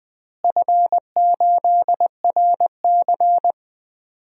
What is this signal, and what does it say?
Morse code, heard as a keyed tone, about 20 wpm, 700 Hz. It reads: F8RC
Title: DXCC-100 Call Signs - 4 Characters - Repeated 2x - 20wpm